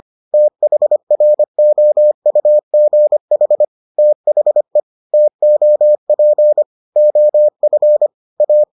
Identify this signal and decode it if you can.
Morse code, heard as a keyed tone, about 25 wpm, 600 Hz. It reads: THROUGH THE TOP OF A